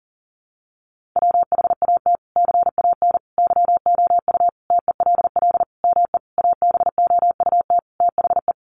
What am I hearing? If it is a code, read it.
Morse code, heard as a keyed tone, about 40 wpm, 700 Hz. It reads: WHAT CAN YOU TELL ME ABOUT THE
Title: Sentences from Top 1500 Words 40wpm